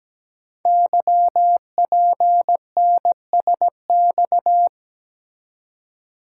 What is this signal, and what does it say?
Morse code, heard as a keyed tone, about 17 wpm, 700 Hz. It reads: YPNSX
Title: Code Groups - 5 Letter:Numbers - 17wpm